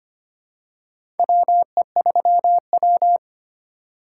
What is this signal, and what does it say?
Morse code, heard as a keyed tone, about 25 wpm, 700 Hz. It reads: WE3W
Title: USA Call Signs - 4 Characters - 4x Character Spacing - 25wpm